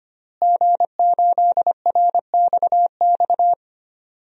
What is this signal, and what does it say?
Morse code, heard as a keyed tone, about 25 wpm, 700 Hz. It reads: G8RXX
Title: DXCC-100 Call Signs - 5 Characters - Repeated 2x - 25wpm